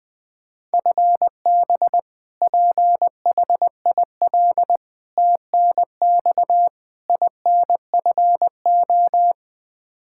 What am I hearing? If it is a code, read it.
Morse code, heard as a keyed tone, about 20 wpm, 700 Hz. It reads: FB PHIL TNX INFO
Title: Practice QSO - Exchange 3 - More Info - 20wpm